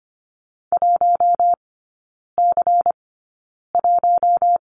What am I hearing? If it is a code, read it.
Morse code, heard as a keyed tone, about 25 wpm, spaced as for 9 wpm, 700 Hz. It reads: 1C1
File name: course / 25:10wpm / 093 Lesson 16e - Letter c - Three Characters 25wpm